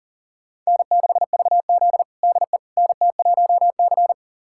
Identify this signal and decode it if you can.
Morse code, heard as a keyed tone, about 40 wpm, 700 Hz. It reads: N6VZ DE NT1C
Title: USA Callsign DE Callsign - 4 Characters - 40wpm